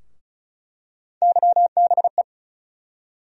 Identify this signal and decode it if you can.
Morse code, heard as a keyed tone, about 34 wpm, 700 Hz. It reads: YBE
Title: USA Call Sign Postfixes - 3 Letters 35wpm